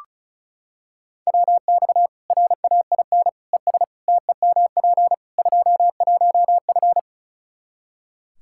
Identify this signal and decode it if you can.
Morse code, heard as a keyed tone, about 36 wpm, 700 Hz. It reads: WX RAIN ES TEMP 21F
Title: Practice QSO - Exchange 2 - Info - 35wpm